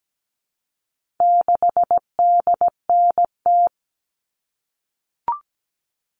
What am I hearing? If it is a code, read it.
Morse code, heard as a keyed tone, about 17 wpm, 700 Hz. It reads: 6DNT
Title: All Cut Numbers - 4-Digit 17wpm